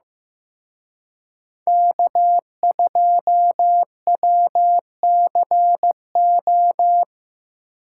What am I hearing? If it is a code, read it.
Morse code, heard as a keyed tone, about 18 wpm, 700 Hz. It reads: K2WCO
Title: Speed Racing - USA 1x3 Call Signs 15wpm